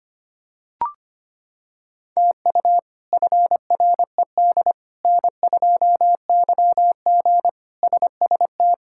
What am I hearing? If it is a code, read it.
Morse code, heard as a keyed tone, about 25 wpm, 700 Hz. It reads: TU FRED N2YG SST
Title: K1USN SST QSO Practice - 25wpm